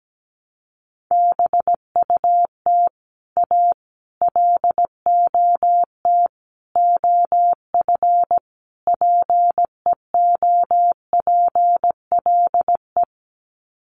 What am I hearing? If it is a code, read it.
Morse code, heard as a keyed tone, about 17 wpm, 700 Hz. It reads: BUT A LOT OF PEOPLE
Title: Sets of 5 Words Encore2 17wpm